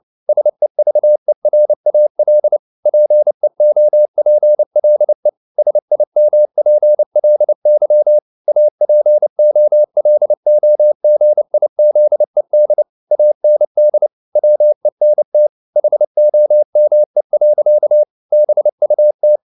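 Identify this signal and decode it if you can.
Morse code, heard as a keyed tone, about 29 wpm, 600 Hz. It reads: SEVERAL PEOPLE SIMPLY APOLOGIZED AND WENT HOME. BUT
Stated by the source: Morse code at 29 wpm, sending Ken Follett's novel Never in English, from CW med Gnister.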